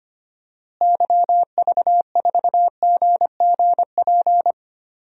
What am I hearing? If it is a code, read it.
Morse code, heard as a keyed tone, about 25 wpm, 700 Hz. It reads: YV4GGP
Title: DXCC-100 Call Signs - 6 Characters - Repeated 2x - 25wpm